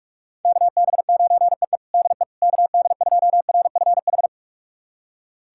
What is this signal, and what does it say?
Morse code, heard as a keyed tone, about 41 wpm, 700 Hz. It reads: KB9EE DE KD2RFH
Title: USA Callsign DE Callsign 45wpm